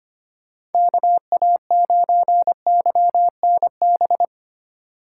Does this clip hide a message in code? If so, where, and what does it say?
Yes: Morse code from 0.7 to 4.3 s, heard as a keyed tone, about 33 wpm, 700 Hz. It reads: KA9YNB